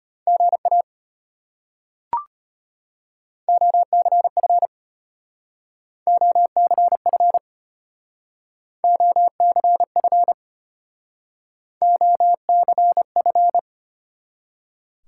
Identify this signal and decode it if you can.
Morse code, heard as a keyed tone, about 32 wpm, 700 Hz. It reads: GA OCF OCF OCF OCF